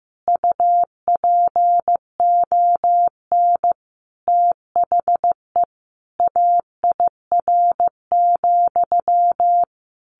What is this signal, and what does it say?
Morse code, heard as a keyed tone, about 15 wpm, 700 Hz. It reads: UPON THE AIR,
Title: A Sound Of Thunder 15wpm